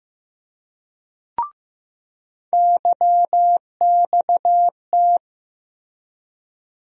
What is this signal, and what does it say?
Morse code, heard as a keyed tone, about 15 wpm, 700 Hz. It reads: YXT
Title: USA Call Sign Postfixes - 3 Letters 15wpm